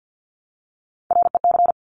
Morse code, heard as a keyed tone, about 50 wpm, 700 Hz. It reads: REC
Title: Top Three Letter Combinations in Words 50wpm